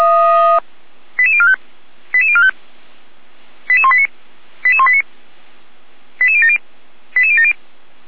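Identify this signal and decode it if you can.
Five-tone selective call (ZVEI), heard as quick runs of five tones, tones of about 80 ms. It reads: 88045 88045 88189 88189 88070 88070
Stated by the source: ZVEI five-tone selective call received on an SDR, Feuerwehr Orlamünde.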